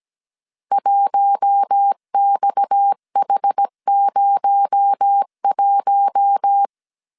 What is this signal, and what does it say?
Morse code, heard as a keyed tone, about 17 wpm, 800 Hz. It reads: 1XH01
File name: CW / DET / 1 6 1500C S A 16wpm 2635